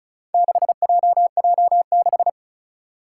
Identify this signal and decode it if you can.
Morse code, heard as a keyed tone, about 36 wpm, 700 Hz. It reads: 6JJ6
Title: Four Challenging Letter-Number Combinations 35wpm